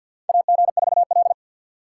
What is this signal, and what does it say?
Morse code, heard as a keyed tone, about 51 wpm, 700 Hz. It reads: AG4L
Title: USA Call Signs - 4 Characters - 3x Character Spacing - 50wpm